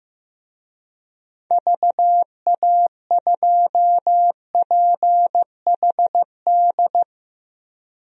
Morse code, heard as a keyed tone, about 16 wpm, 700 Hz. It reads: VA2PHD
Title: DXCC-100 Call Signs - 6 Characters - Repeated 2x - 15wpm